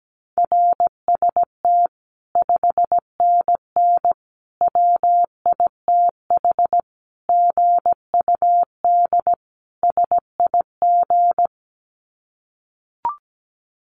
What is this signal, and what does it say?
Morse code, heard as a keyed tone, about 17 wpm, 700 Hz. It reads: RST 5NN WITH GUD SIG
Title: Common QSO Phrases 17wpm